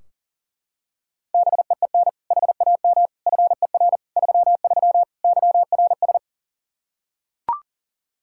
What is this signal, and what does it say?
Morse code, heard as a keyed tone, about 37 wpm, 700 Hz. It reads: BEEN HAM FER 33 YRS
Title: Practice QSO - Exchange 3 - More Info - 40wpm